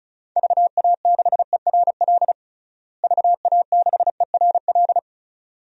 Morse code, heard as a keyed tone, about 36 wpm, 700 Hz. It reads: VA6ERL VA6ERL
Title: DXCC-100 Call Signs - 6 Characters - Repeated 2x - 35wpm